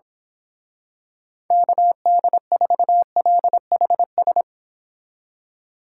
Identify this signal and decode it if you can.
Morse code, heard as a keyed tone, about 26 wpm, 700 Hz. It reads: KD4LHS
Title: Speed Racing - USA 2x3 Call Signs 22wpm